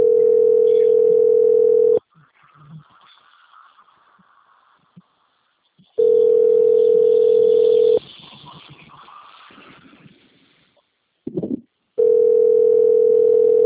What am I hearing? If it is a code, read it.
Ringback tone (North America), 440 + 480 Hz.